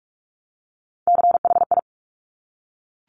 Morse code, heard as a keyed tone, about 46 wpm, 700 Hz. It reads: CHI
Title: Top Three Letter Combinations in Words 45wpm